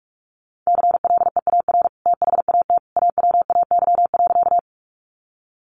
Morse code, heard as a keyed tone, about 44 wpm, 700 Hz. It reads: CLEAR THAT AWAY.
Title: Sentences from Top 600 Words 45wpm